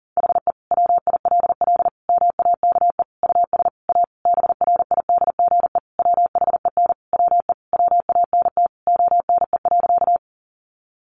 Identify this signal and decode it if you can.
Morse code, heard as a keyed tone, about 40 wpm, 700 Hz. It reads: HE WILL MAKE US A BRIDGE WHEN WE WANT ONE.
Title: Sentences from Top 1200 Words 40wpm